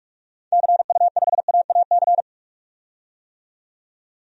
Morse code, heard as a keyed tone, about 44 wpm, 700 Hz. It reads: CU5AAC
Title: DXCC-100 Call Signs - 6 Characters - 45wpm